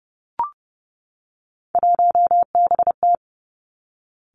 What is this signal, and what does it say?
Morse code, heard as a keyed tone, about 30 wpm, 700 Hz. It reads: 1BT
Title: Three Challenging Letter-Number Combinations 30wpm